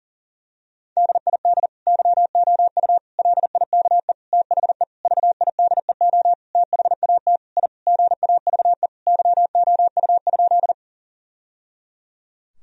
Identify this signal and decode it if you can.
Morse code, heard as a keyed tone, about 40 wpm, 700 Hz. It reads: DID YOU LIKE THE VIDEO THAT I GAVE YOU?
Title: Sentences from Top 1000 Words - Review Part 4 - 40wpm